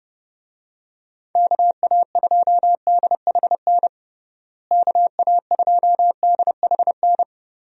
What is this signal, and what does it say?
Morse code, heard as a keyed tone, about 30 wpm, 700 Hz. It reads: KA2DHN KA2DHN
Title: USA Call Signs - 6 Characters - Repeated 2x - 30wpm